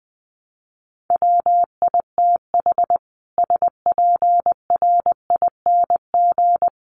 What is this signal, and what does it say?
Morse code, heard as a keyed tone, about 20 wpm, 700 Hz. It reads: WITH SPRING
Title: Sentences from Top 1200 Words 20wpm